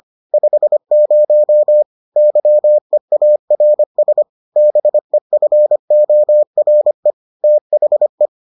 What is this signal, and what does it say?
Morse code, heard as a keyed tone, about 25 wpm, 600 Hz. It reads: 50 YEARS BEFORE THE